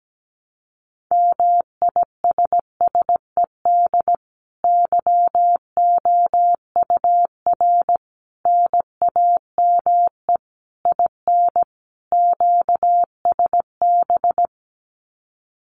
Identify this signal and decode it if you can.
Morse code, heard as a keyed tone, about 17 wpm, 700 Hz. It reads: MISSED YOUR NAME IN QSB
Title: Common QSO Phrases 17wpm